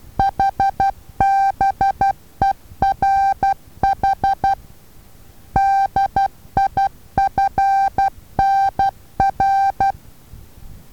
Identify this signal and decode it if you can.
Morse code, heard as a keyed tone, about 12 wpm, 780 Hz. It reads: HBERH DIFNR
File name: Taviro HA6VA B